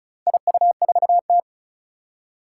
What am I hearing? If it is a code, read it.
Morse code, heard as a keyed tone, about 36 wpm, 700 Hz. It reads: IU4T
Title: Four Challenging Letter-Number Combinations 35wpm